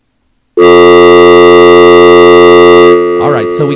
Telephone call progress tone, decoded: dial tone